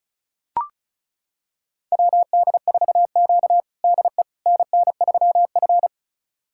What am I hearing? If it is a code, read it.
Morse code, heard as a keyed tone, about 36 wpm, 700 Hz. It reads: WD4Q DE NN3F
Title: USA Callsign DE Callsign - 4 Characters - 35wpm